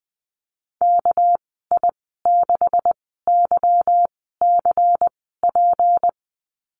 Morse code, heard as a keyed tone, about 19 wpm, spaced as for 14 wpm, 700 Hz. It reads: KI6YCP
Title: 219 Lesson 35g - Number 0 - Call Signs 20wpm